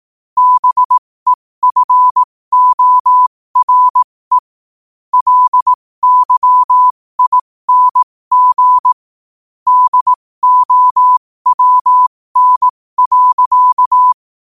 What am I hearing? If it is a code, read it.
Morse code, heard as a keyed tone, about 18 wpm, spaced as for 15 wpm, 1000 Hz. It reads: BEFORE LYING DOWN.
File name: qotdfiles / 15wpm